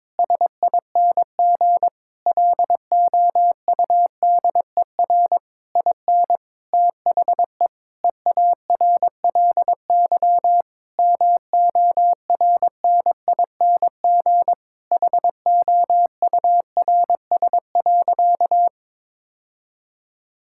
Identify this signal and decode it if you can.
Morse code, heard as a keyed tone, about 22 wpm, 700 Hz. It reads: SING LOUDER IN THE EARLY MORNING HOURS.
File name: Difficult Sentences with Hints - 22wpm